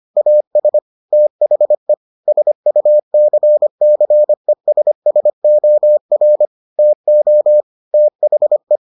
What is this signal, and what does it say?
Morse code, heard as a keyed tone, about 25 wpm, 600 Hz. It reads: AS THE SUCCESSOR TO THE